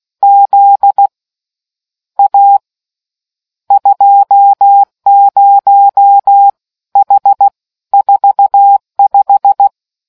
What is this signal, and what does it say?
Morse code, heard as a keyed tone, about 16 wpm, spaced as for 13 wpm, 800 Hz. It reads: Z A 20H45